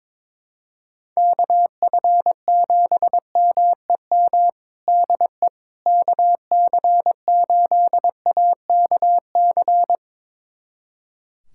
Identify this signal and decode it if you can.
Morse code, heard as a keyed tone, about 22 wpm, 700 Hz. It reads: KF7MEM DE KC8AKC
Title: USA Callsign DE Callsign - 6 Characters - 22wpm